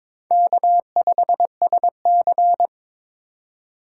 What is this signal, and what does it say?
Morse code, heard as a keyed tone, about 22 wpm, 700 Hz. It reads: K5SC
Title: USA Call Signs - 3x Character Spacing - 22wpm